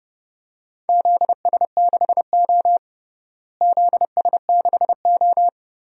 Morse code, heard as a keyed tone, about 32 wpm, 700 Hz. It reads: ZS6O ZS6O
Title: DXCC-100 Call Signs - 4 Characters - Repeated 3x - 30wpm